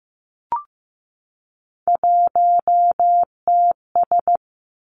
Morse code, heard as a keyed tone, about 15 wpm, 700 Hz. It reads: 1TS
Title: Three Challenging Letter-Number Combinations 15wpm